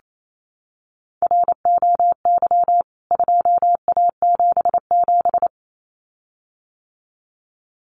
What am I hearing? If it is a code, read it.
Morse code, heard as a keyed tone, about 28 wpm, 700 Hz. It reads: ROY 2A77
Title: CWT Exchange 28wpm